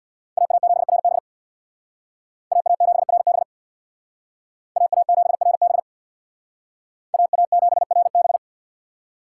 Morse code, heard as a keyed tone, about 64 wpm, 700 Hz. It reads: AA7RB AA7RB AA7RB AA7RB